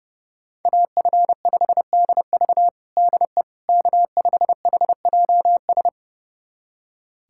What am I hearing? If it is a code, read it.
Morse code, heard as a keyed tone, about 31 wpm, 700 Hz. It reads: AF5DV DE K5HJS
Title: USA Callsign DE Callsign - 5 Characters - 30wpm